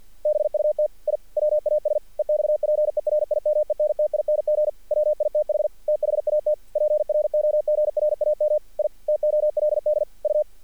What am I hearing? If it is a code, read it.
Morse code, heard as a keyed tone, about 48 wpm, 600 Hz. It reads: BUT I WAS EXPERIMENTING WITH THAT PROGRAM I TOLD U
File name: archived / CRSnet